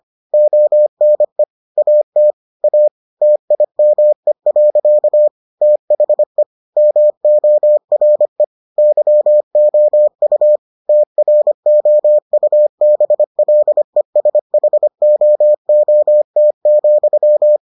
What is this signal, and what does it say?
Morse code, heard as a keyed tone, about 25 wpm, 600 Hz. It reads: ONE AT A TIME. THE MORE YOU TROUBLESHOOT,